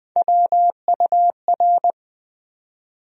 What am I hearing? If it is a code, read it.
Morse code, heard as a keyed tone, about 21 wpm, 700 Hz. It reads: WUR
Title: USA Call Sign Postfixes - 3 Letters 20wpm